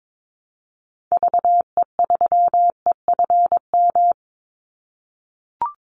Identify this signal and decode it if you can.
Morse code, heard as a keyed tone, about 22 wpm, 700 Hz. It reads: VE3EFM
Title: Call Signs - 22wpm